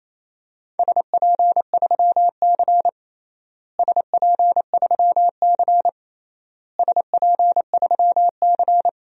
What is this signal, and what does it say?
Morse code, heard as a keyed tone, about 28 wpm, 700 Hz. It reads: SP3C SP3C SP3C
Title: DXCC-100 Call Signs - 4 Characters - Repeated 3x - 28wpm